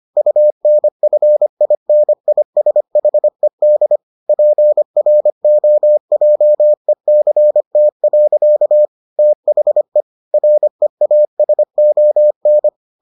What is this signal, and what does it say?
Morse code, heard as a keyed tone, about 25 wpm, 600 Hz. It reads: UNFINISHED PROJECT. THE REASON